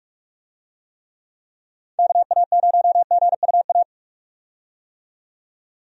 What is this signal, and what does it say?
Morse code, heard as a keyed tone, about 46 wpm, 700 Hz. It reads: KA0GUA
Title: USA Call Signs - 6 Characters - 45wpm